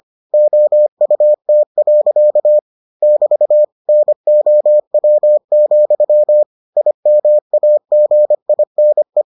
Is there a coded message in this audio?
Yes: Morse code from 0.3 to 9.2 s, heard as a keyed tone, about 25 wpm, 600 Hz. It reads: OUT. <BT> NOW, IMAGINE